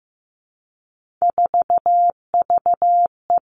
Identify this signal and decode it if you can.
Morse code, heard as a keyed tone, about 15 wpm, 700 Hz. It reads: 4VE
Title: Three Challenging Letter-Number Combinations 15wpm